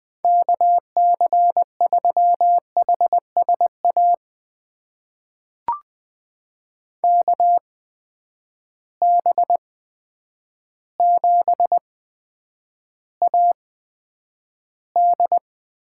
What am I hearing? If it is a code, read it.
Morse code, heard as a keyed tone, about 20 wpm, spaced as for 9 wpm, 700 Hz. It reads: KC3HSA KB7AD
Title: USA Call Signs - 6 Characters - 8x Character Spacing - 20wpm